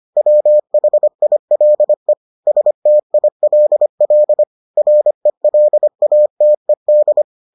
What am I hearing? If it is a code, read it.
Morse code, heard as a keyed tone, about 25 wpm, 600 Hz. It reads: WHILE STILL RELATED